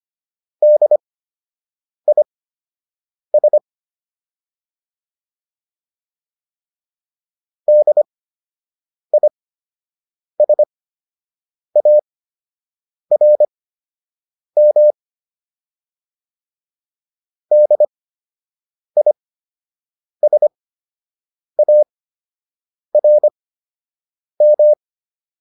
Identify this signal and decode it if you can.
Morse code, heard as a keyed tone, about 25 wpm, spaced as for 7 wpm, 600 Hz. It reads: DIS DISARM DISARM